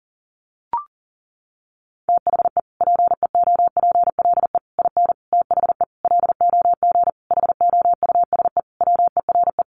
Morse code, heard as a keyed tone, about 40 wpm, 700 Hz. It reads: THE PEOPLE IN THE LOG HOUSE WERE
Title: Sentences from Top 1400 Words 40wpm